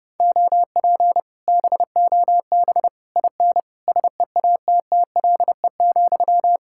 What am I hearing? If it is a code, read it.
Morse code, heard as a keyed tone, about 29 wpm, 700 Hz. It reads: OP BOB IN SEATTLE,